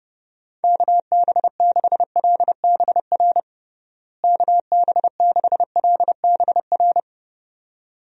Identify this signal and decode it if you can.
Morse code, heard as a keyed tone, about 29 wpm, 700 Hz. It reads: KB6LBR KB6LBR
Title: USA Call Signs - 6 Characters - Repeated 2x - 30wpm